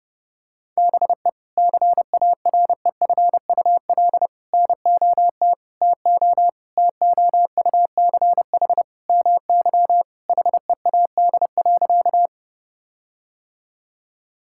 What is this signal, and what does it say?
Morse code, heard as a keyed tone, about 30 wpm, 700 Hz. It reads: BE CAREFUL NOT TO TOUCH MY HEAD.
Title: Sentences from Top 1600 Words 30wpm